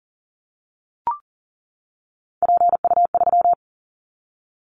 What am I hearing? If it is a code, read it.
Morse code, heard as a keyed tone, about 40 wpm, 700 Hz. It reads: PU3
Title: Callsign Prefixes for Top 100 DXCC - 40wpm